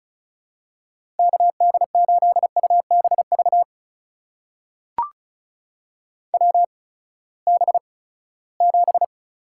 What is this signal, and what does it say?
Morse code, heard as a keyed tone, about 36 wpm, spaced as for 20 wpm, 700 Hz. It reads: KD8UBV WB7
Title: USA Call Signs - 6 Characters - 8x Character Spacing - 35wpm